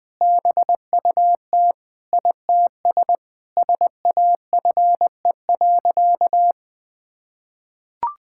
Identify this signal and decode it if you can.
Morse code, heard as a keyed tone, about 20 wpm, 700 Hz. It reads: BUT ITS SAFE.
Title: Sentences with 2-Word Phrases - 20wpm